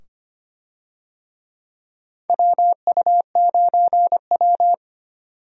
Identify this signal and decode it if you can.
Morse code, heard as a keyed tone, about 25 wpm, 700 Hz. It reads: WU9W
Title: USA Call Signs - 4 Characters - 25wpm